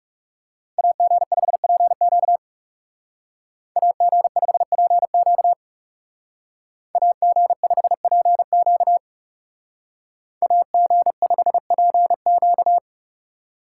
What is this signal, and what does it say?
Morse code, heard as a keyed tone, about 36 wpm, 700 Hz. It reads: AG5PQ AG5PQ AG5PQ AG5PQ
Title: Speed Racing - USA 2x2 Call Signs 30wpm